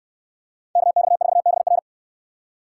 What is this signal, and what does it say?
Morse code, heard as a keyed tone, about 69 wpm, 700 Hz. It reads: KX4BR